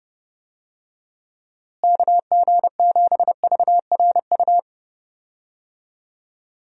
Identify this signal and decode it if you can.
Morse code, heard as a keyed tone, about 30 wpm, 700 Hz. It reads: KG7VRU
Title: USA Call Signs - 6 Characters - Repeated 3x - 30wpm